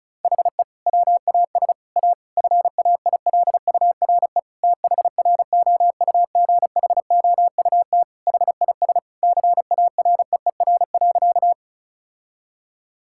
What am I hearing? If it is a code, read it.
Morse code, heard as a keyed tone, about 36 wpm, 700 Hz. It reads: HE WAS A FAILURE THROUGHOUT HIS CAREER.
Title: Sentences from Top 1000 Words - Review Part 6 - 35wpm